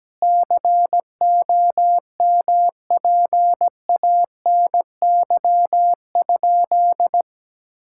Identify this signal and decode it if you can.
Morse code, heard as a keyed tone, about 17 wpm, 700 Hz. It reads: COMPANY?
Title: Sentences from Top 1600 Words 17wpm